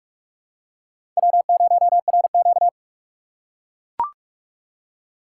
Morse code, heard as a keyed tone, about 43 wpm, 700 Hz. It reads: W0RQ
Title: USA Call Signs - 4 Characters - 2x Character Spacing - 45wpm